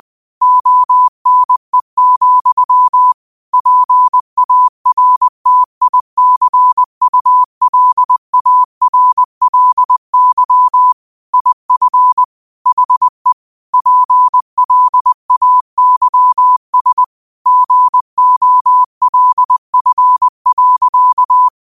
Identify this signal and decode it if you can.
Morse code, heard as a keyed tone, about 20 wpm, 1000 Hz. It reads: ONE, PARTICULARLY IF HE PLAYS GOLF.
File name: qotdfiles / 20wpm